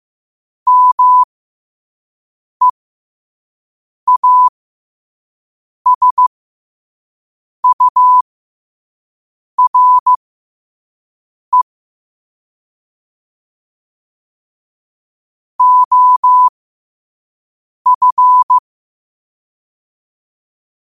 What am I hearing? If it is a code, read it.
Morse code, heard as a keyed tone, about 15 wpm, spaced as for 5 wpm, 1000 Hz. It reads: MEASURE OF
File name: qotdfiles / 5wpm